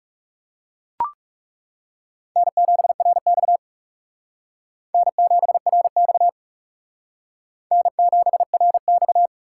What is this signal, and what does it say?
Morse code, heard as a keyed tone, about 35 wpm, 700 Hz. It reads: N7RX N7RX N7RX
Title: Speed Racing - USA 1x2 Call Signs 30wpm